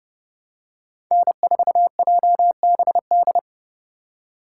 Morse code, heard as a keyed tone, about 32 wpm, 700 Hz. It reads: N4JBD
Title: USA Call Signs - 5 Characters - 3x Character Spacing - 30wpm